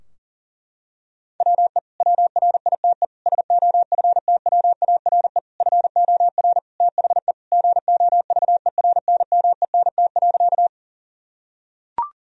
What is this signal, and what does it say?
Morse code, heard as a keyed tone, about 40 wpm, 700 Hz. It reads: WE WRITE SOFTWARE FOR THE GOVERNMENT.
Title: Sentences from Top 1000 Words - Review Part 3 - 40wpm